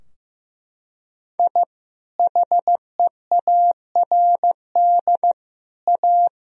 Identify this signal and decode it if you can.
Morse code, heard as a keyed tone, about 15 wpm, 700 Hz. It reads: I HEARD A
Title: Sentences from Top 1000 Words - Review Part 3 - 15wpm